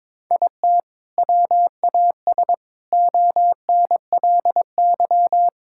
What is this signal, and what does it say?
Morse code, heard as a keyed tone, about 22 wpm, 700 Hz. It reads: IT WAS ONLY
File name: Sentences with 7-Word Phrases - 22wpm